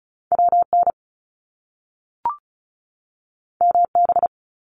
Morse code, heard as a keyed tone, about 35 wpm, 700 Hz. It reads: WN MB
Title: Top Two Letter Combinations in Words 35wpm